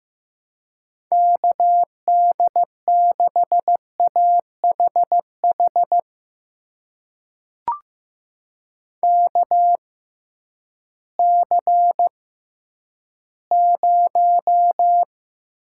Morse code, heard as a keyed tone, about 16 wpm, spaced as for 10 wpm, 700 Hz. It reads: KD6AHH KC0
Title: USA Call Signs - 6x Character Spacing - 15wpm